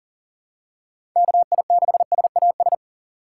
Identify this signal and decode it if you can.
Morse code, heard as a keyed tone, about 40 wpm, 700 Hz. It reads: KI6SAS
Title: USA Call Signs - 6 Characters - 4x Character Spacing - 40wpm